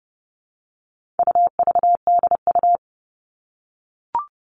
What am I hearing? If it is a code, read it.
Morse code, heard as a keyed tone, about 30 wpm, 700 Hz. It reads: UVDU
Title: All Cut Numbers - 4-Digit 30wpm